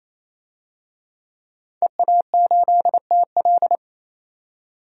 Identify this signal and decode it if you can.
Morse code, heard as a keyed tone, about 28 wpm, 700 Hz. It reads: EA8TL
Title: DXCC-100 Call Signs - 5 Characters - 28wpm